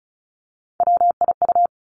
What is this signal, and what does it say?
Morse code, heard as a keyed tone, about 36 wpm, 700 Hz. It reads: WIU
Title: Three Challenging Letter-Number Combinations 35wpm